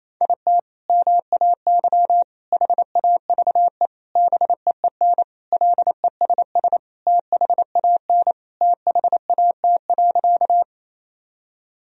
Morse code, heard as a keyed tone, about 28 wpm, 700 Hz. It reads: IT MAY HAVE BEEN LESS THAN THAT.
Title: Sentences from Top 1000 Words - Review Part 2 - 28wpm